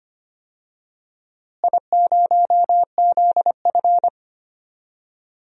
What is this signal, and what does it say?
Morse code, heard as a keyed tone, about 25 wpm, 700 Hz. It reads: I0ZF
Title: DXCC-100 Call Signs - 4 Characters - Repeated 2x - 25wpm